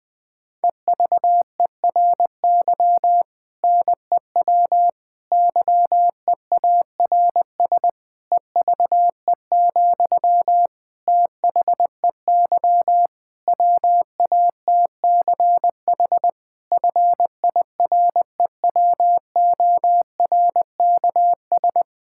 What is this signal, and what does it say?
Morse code, heard as a keyed tone, about 20 wpm, 700 Hz. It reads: EVERY NEW YEARS EVE, THEY WATCH FIREWORKS
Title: Difficult Sentences with Hints - 20wpm